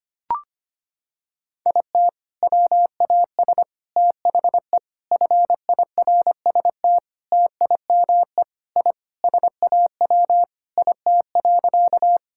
Morse code, heard as a keyed tone, about 25 wpm, 700 Hz. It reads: IT WAS THE FIRST TIME I SAW IT.
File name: practice / 25wpm / Sentences with 3-Word Phrases - 25wpm